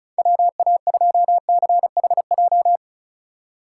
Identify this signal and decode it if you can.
Morse code, heard as a keyed tone, about 36 wpm, 700 Hz. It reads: WA2CHJ
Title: USA Call Signs - 6 Characters - 3x Character Spacing - 35wpm